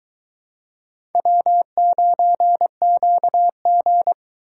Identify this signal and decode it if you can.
Morse code, heard as a keyed tone, about 24 wpm, 700 Hz. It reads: W9QG